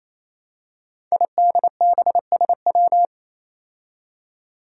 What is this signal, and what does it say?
Morse code, heard as a keyed tone, about 27 wpm, 700 Hz. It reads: IDBSW
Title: Five Challenging Letter-Number Combinations 28wpm